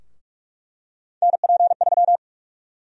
Morse code, heard as a keyed tone, about 46 wpm, 700 Hz. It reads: NP3
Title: USA Call Sign Prefixes - 45wpm